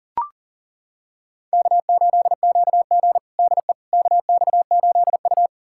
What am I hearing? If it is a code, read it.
Morse code, heard as a keyed tone, about 40 wpm, 700 Hz. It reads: K8QG DE KX8U
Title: USA Callsign DE Callsign - 4 Characters - 40wpm